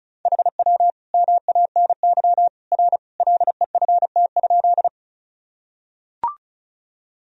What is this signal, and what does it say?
Morse code, heard as a keyed tone, about 34 wpm, 700 Hz. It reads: HW MANY R LEFT?
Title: Sentences from Top 300 Words and Abbreviations 35wpm